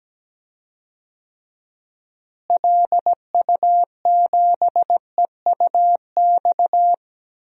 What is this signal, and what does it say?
Morse code, heard as a keyed tone, about 17 wpm, 700 Hz. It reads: LU7EUX
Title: DXCC-100 Call Signs - 6 Characters - Repeated 3x - 17wpm